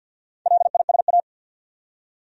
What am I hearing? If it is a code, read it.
Morse code, heard as a keyed tone, about 53 wpm, 700 Hz. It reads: LISA
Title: USA Common Names - Max 5 Letters - 50wpm